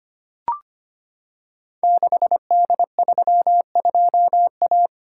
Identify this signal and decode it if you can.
Morse code, heard as a keyed tone, about 25 wpm, 700 Hz. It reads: 6D32A
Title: Code Groups - 5 Letter:Numbers - 25wpm